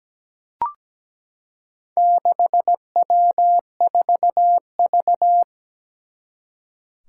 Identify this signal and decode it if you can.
Morse code, heard as a keyed tone, about 17 wpm, 700 Hz. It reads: 6W4V
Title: Four Challenging Letter-Number Combinations 17wpm